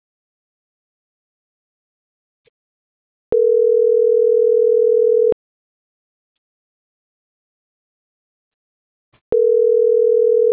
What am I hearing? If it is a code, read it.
Ringback tone (North America), 440 + 480 Hz.